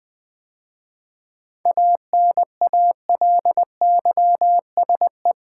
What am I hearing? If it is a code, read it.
Morse code, heard as a keyed tone, about 20 wpm, 700 Hz. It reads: ANALYSE
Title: 7 Letter Words - First 3 Letters Given - 20wpm